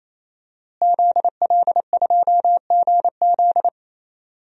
Morse code, heard as a keyed tone, about 28 wpm, 700 Hz. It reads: ZL2GZ
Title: DXCC-100 Call Signs - 5 Characters - Repeated 3x - 28wpm